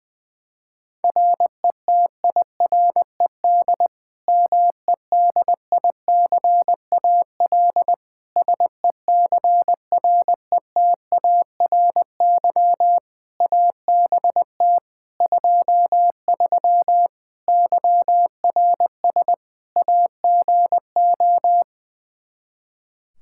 Morse code, heard as a keyed tone, about 20 wpm, 700 Hz. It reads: RETIRED MEDICAL SECRETARY ABT 23 YRS AGO
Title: K1USN SST QSO Practice 20wpm